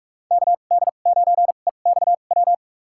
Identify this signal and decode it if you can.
Morse code, heard as a keyed tone, about 46 wpm, spaced as for 32 wpm, 700 Hz. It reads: KD9EXW
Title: USA Call Signs - 2x Character Spacing - 45wpm